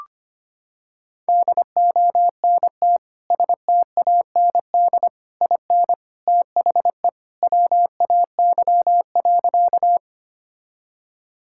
Morse code, heard as a keyed tone, about 25 wpm, 700 Hz. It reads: DONT STAND IN THE WAY.
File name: practice / 25wpm / Sentences with 3-Word Phrases - 25wpm